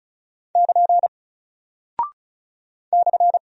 Morse code, heard as a keyed tone, about 34 wpm, 700 Hz. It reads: <KN> /
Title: Single Character (Letter:Number:Punctuation:Prosign) Alt 35wpm